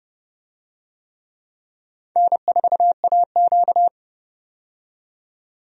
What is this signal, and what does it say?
Morse code, heard as a keyed tone, about 30 wpm, 700 Hz. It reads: N4AQ
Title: USA Call Signs - 4 Characters - 30wpm